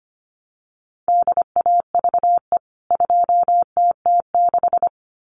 Morse code, heard as a keyed tone, about 25 wpm, 700 Hz. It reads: DAVE 2TT6
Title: CWT Exchange 25wpm